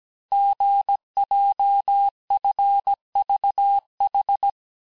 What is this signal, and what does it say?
Morse code, heard as a keyed tone, about 17 wpm, 800 Hz. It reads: GJFVH
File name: cours-du-lundi files / cours-1587-texte-chiffre-vitesse-16MPM